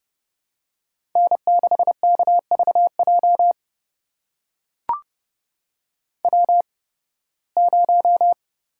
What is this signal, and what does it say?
Morse code, heard as a keyed tone, about 31 wpm, spaced as for 20 wpm, 700 Hz. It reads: N6KVJ W0
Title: USA Call Signs - 5 Characters - 8x Character Spacing - 30wpm